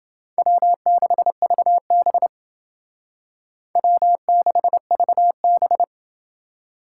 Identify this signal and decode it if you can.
Morse code, heard as a keyed tone, about 28 wpm, 700 Hz. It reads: W6VB W6VB